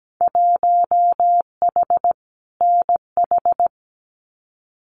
Morse code, heard as a keyed tone, about 17 wpm, 700 Hz. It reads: 1H NH
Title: Winter Field Day Exchanges 17wpm